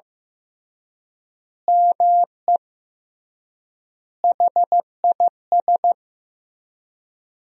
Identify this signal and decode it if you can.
Morse code, heard as a keyed tone, about 15 wpm, 700 Hz. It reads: ME HIS
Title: 3x Word-Spacing - Sentences from Top 200 Words - 15wpm